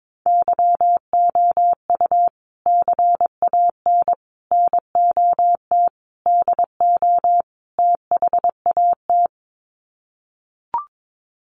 Morse code, heard as a keyed tone, about 22 wpm, 700 Hz. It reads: YOU CAN NOT DO THAT
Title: Sets of 5 Words 22wpm